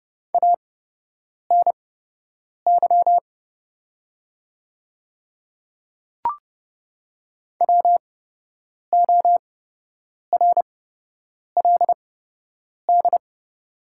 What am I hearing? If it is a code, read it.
Morse code, heard as a keyed tone, about 30 wpm, spaced as for 8 wpm, 700 Hz. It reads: ANY WORLD
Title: 8x Character Spacing - Top 300 Words - 30wpm